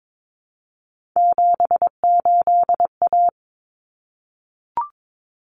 Morse code, heard as a keyed tone, about 22 wpm, 700 Hz. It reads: 78A
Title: Common Cut Numbers - 3-Digit 22wpm